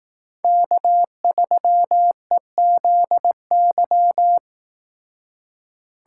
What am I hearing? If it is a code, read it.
Morse code, heard as a keyed tone, about 18 wpm, 700 Hz. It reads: K3EZY
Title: Speed Racing - USA 1x3 Call Signs 15wpm